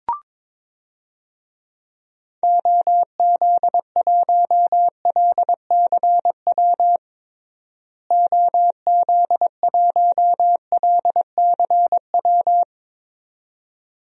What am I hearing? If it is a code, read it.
Morse code, heard as a keyed tone, about 22 wpm, 700 Hz. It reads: OZ1LCW OZ1LCW
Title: DXCC-100 Call Signs - 6 Characters - Repeated 2x - 22wpm